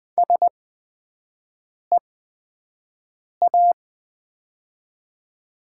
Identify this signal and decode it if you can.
Morse code, heard as a keyed tone, about 20 wpm, spaced as for 5 wpm, 700 Hz. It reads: SEA
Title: 8x Character Spacing - Top 500 Words - 20wpm